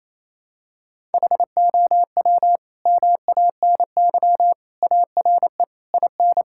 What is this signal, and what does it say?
Morse code, heard as a keyed tone, about 28 wpm, 700 Hz. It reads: HOW MANY ARE IN
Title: Sentences from Top 1000 Words - Review Part 5 - 28wpm